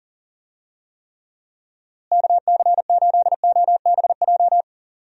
Morse code, heard as a keyed tone, about 38 wpm, 700 Hz. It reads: KC8OBJ
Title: USA Call Signs - 6 Characters - Repeated 2x - 40wpm